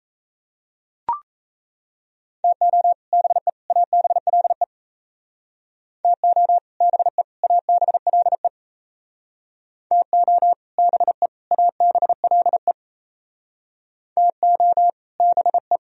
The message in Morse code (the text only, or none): TO BE ABLE TO BE ABLE TO BE ABLE TO BE